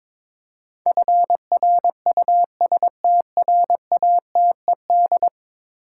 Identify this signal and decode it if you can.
Morse code, heard as a keyed tone, about 22 wpm, 700 Hz. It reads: FRUSTRATED
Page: morse-code-ninja-practice-22wpm